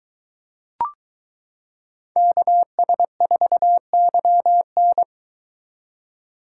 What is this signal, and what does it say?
Morse code, heard as a keyed tone, about 22 wpm, 700 Hz. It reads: KS4YN